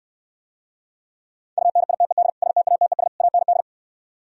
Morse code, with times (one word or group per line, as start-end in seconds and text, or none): 1.6-2.3 UNITED
2.4-3.1 STATES
3.2-3.6 AND